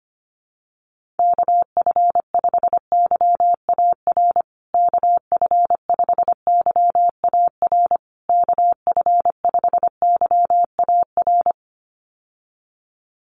Morse code, heard as a keyed tone, about 25 wpm, 700 Hz. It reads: KF5YAR KF5YAR KF5YAR
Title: USA Call Signs Repeated 3x 25wpm